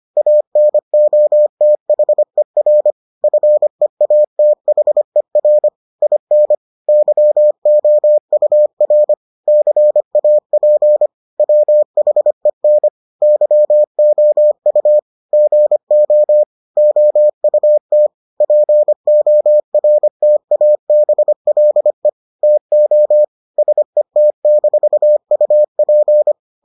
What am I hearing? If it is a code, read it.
Morse code, heard as a keyed tone, about 25 wpm, 600 Hz. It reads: ANOTHER FEATHER IN YOUR CAP WHEN YOU GO OUT PORTABLE TO SET-UP